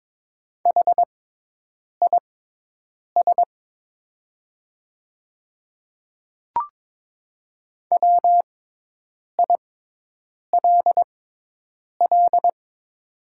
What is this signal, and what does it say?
Morse code, heard as a keyed tone, about 22 wpm, spaced as for 8 wpm, 700 Hz. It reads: HIS WILL
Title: 6x Character Spacing - Top 500 Words - 22wpm